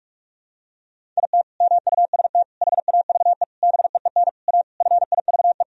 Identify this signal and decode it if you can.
Morse code, heard as a keyed tone, about 46 wpm, 700 Hz. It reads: IT MUST HAVE BEEN A FIVE
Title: Sentences from Top 1000 Words - Review Part 2 - 45wpm